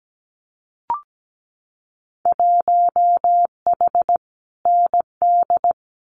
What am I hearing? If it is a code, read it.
Morse code, heard as a keyed tone, about 17 wpm, 700 Hz. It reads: 1H ND
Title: Winter Field Day Exchanges 17wpm